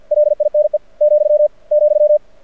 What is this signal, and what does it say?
Morse code, heard as a keyed tone, about 51 wpm, 600 Hz. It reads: LINE , ,
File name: archived / CRSnet